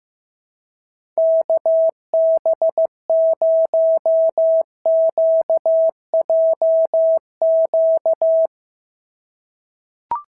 Morse code, heard as a keyed tone, about 15 wpm, 650 Hz. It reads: KB0QJQ
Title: SOTA QSO Practice 15wpm